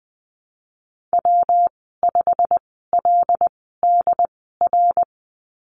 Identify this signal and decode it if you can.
Morse code, heard as a keyed tone, about 20 wpm, spaced as for 15 wpm, 700 Hz. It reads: W5LDR
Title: 122 Lesson 20g - Number 3 - Call Signs 20wpm